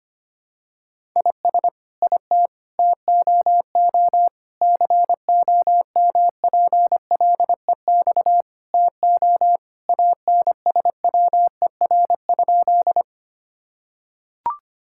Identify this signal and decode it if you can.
Morse code, heard as a keyed tone, about 25 wpm, 700 Hz. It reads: IS IT TOO COMPLEX TO ANSWER?
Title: Sentences from Top 1000 Words - Review Part 6 - 25wpm